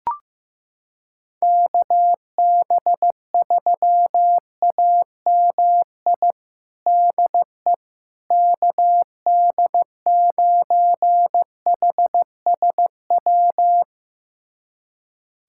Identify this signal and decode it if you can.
Morse code, heard as a keyed tone, about 16 wpm, 700 Hz. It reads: KB3AMI DE KD9HSW
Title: USA Callsign DE Callsign - 6 Characters - 15wpm